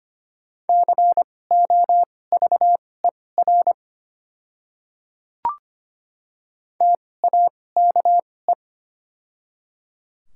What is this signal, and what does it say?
Morse code, heard as a keyed tone, about 25 wpm, spaced as for 18 wpm, 700 Hz. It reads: COVER TAKE
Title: 2x Character Spacing - Top 500 Words - 25wpm